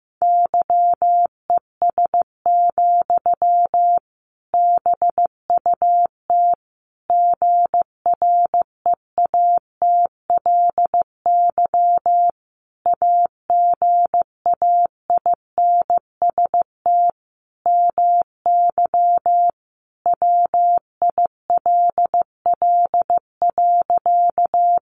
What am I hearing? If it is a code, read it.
Morse code, heard as a keyed tone, about 15 wpm, 700 Hz. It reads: YES, BUT GREATLY AGAINST MY WILL.
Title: Sentences from Top 1400 Words 15wpm